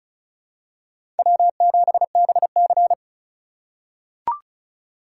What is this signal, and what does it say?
Morse code, heard as a keyed tone, about 34 wpm, 700 Hz. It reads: W7BC
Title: USA Call Signs - 4x Character Spacing - 35wpm